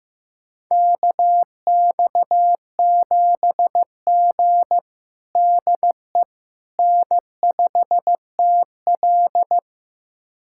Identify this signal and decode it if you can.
Morse code, heard as a keyed tone, about 15 wpm, 700 Hz. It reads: KX7G DE N5TL
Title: USA Callsign DE Callsign - 4 Characters - 15wpm